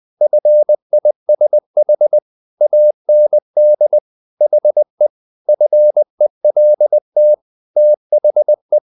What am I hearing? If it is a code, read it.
Morse code, heard as a keyed tone, about 20 wpm, 600 Hz. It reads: FISH AND HE FELT THE
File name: the old man and the sea / the old man and the sea part 1 20wpm